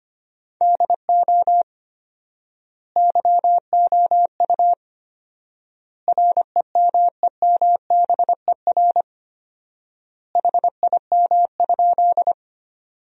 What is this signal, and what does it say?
Morse code, heard as a keyed tone, about 25 wpm, 700 Hz. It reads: DO YOU REMEMBER HIM?
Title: 4x Word-Spacing - Sentences from Top 400 Words - 25wpm